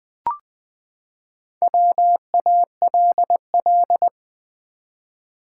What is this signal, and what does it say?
Morse code, heard as a keyed tone, about 20 wpm, 700 Hz. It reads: WALL
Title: Sentences from Top 1100 Words - New Word Drill Alt v2 - 20wpm